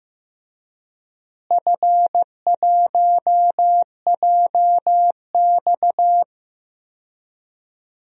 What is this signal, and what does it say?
Morse code, heard as a keyed tone, about 15 wpm, 700 Hz. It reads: F1JX
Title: DXCC-100 Call Signs - 4 Characters - Repeated 3x - 15wpm